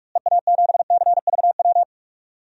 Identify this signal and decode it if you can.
Morse code, heard as a keyed tone, about 46 wpm, 700 Hz. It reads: EA7CVW
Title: DXCC-100 Call Signs - 6 Characters - Repeated 3x - 45wpm